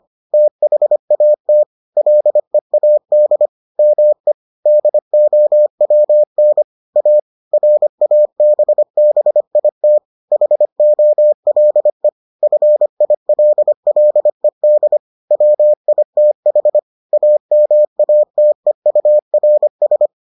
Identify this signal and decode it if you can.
Morse code, heard as a keyed tone, about 25 wpm, 600 Hz. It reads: THAT LEAD ME DOWN A RABBIT HOLE FILLED WITH AMATEURS